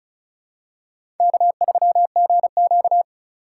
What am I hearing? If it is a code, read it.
Morse code, heard as a keyed tone, about 34 wpm, 700 Hz. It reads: K3GQ
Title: USA Call Signs - 4 Characters - 3x Character Spacing - 35wpm